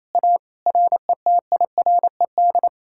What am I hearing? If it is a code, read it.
Morse code, heard as a keyed tone, about 38 wpm, 700 Hz. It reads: A RETIRED